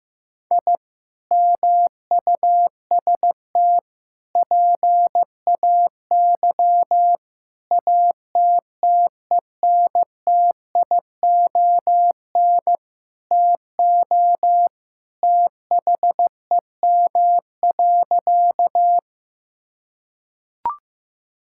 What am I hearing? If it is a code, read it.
Morse code, heard as a keyed tone, about 15 wpm, 700 Hz. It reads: I MUST PAY ATTENTION TO THEM.
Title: Sentences from Top 1000 Words - Review Part 6 - 15wpm